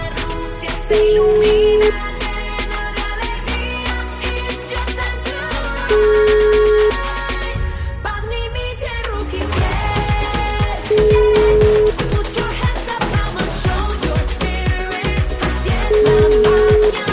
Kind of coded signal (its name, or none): telephone call progress tone